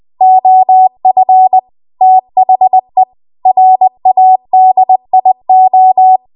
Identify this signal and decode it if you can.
Morse code, heard as a keyed tone, about 20 wpm, 750 Hz. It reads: OF THE RADIO
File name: Archive / 20 WPM / 20WPM